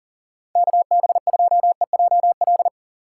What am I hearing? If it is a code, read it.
Morse code, heard as a keyed tone, about 50 wpm, 700 Hz. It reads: KB2EJL